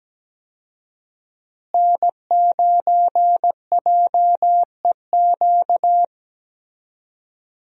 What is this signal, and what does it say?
Morse code, heard as a keyed tone, about 17 wpm, 700 Hz. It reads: N9JEQ
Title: USA Call Signs - 5 Characters - Repeated 3x - 17wpm